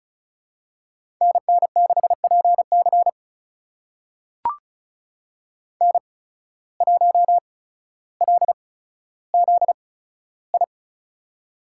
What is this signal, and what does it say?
Morse code, heard as a keyed tone, about 36 wpm, spaced as for 15 wpm, 700 Hz. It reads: NN6PC N1LZI
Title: USA Call Signs - 8x Character Spacing - 35wpm